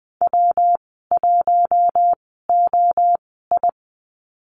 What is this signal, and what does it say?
Morse code, heard as a keyed tone, about 19 wpm, 700 Hz. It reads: W 1 O I
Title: 231 Lesson 37g - Letter x - Call Signs 20wpm